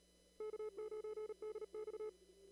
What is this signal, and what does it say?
Morse code, heard as a keyed tone, about 37 wpm, 430 Hz. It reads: K9DX
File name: CW / kt5j wpx cw-op-k5pi